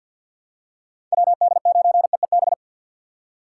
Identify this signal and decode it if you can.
Morse code, heard as a keyed tone, about 52 wpm, 700 Hz. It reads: WD9EEB
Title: USA Call Signs - 6 Characters - 3x Character Spacing - 50wpm